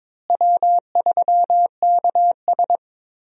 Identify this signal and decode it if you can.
Morse code, heard as a keyed tone, about 22 wpm, 700 Hz. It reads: W3KS
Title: USA Call Signs - 4 Characters - Repeated 3x - 22wpm